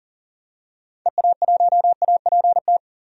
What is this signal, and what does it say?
Morse code, heard as a keyed tone, about 40 wpm, 700 Hz. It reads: EA1APT
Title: DXCC-100 Call Signs - 6 Characters - Repeated 3x - 40wpm